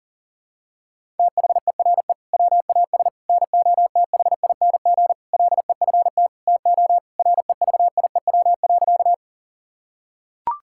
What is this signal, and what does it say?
Morse code, heard as a keyed tone, about 40 wpm, 700 Hz. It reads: THERE WAS NOTHING LEFT TO REVIEW.
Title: Sentences from Top 1000 Words - Review Part 6 - 40wpm